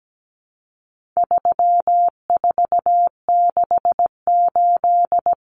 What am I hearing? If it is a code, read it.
Morse code, heard as a keyed tone, about 17 wpm, 700 Hz. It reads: 3468
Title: Numbers - 4-Digit 17wpm